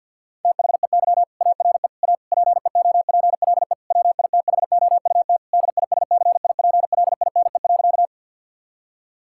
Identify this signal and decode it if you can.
Morse code, heard as a keyed tone, about 50 wpm, 700 Hz. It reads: THEY ARE A PEOPLE WITHOUT DISCIPLINE.
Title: Sentences from Top 1000 Words - Review Part 6 - 50wpm